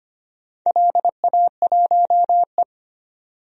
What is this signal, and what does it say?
Morse code, heard as a keyed tone, about 25 wpm, 700 Hz. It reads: LA1E
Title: DXCC-100 Call Signs - 4 Characters - 25wpm